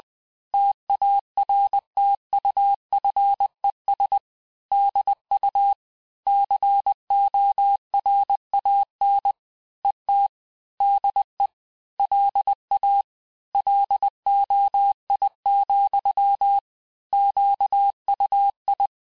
Morse code, heard as a keyed tone, about 20 wpm, 800 Hz. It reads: TARTUFES DU CORAN ET DE LA LOI, QUI
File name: cours-du-mercredi / audios / cours-858-texte-clais-vitesse-19mpm